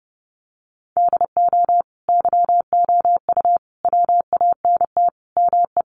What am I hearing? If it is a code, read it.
Morse code, heard as a keyed tone, about 30 wpm, 700 Hz. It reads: DO YOU WANT ME
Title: Sets of 5 Words 30wpm